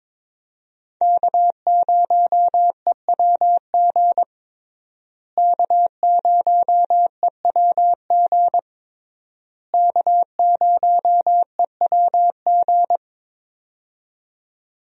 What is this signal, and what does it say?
Morse code, heard as a keyed tone, about 22 wpm, 700 Hz. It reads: K0EWG K0EWG K0EWG
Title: USA Call Signs - 5 Characters - Repeated 3x - 22wpm